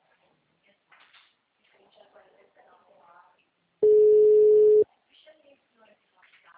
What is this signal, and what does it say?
Ringback tone (Europe, 425 Hz), with speech.